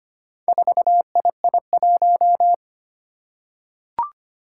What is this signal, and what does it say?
Morse code, heard as a keyed tone, about 25 wpm, 700 Hz. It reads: 4II1
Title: Four Challenging Letter-Number Combinations 25wpm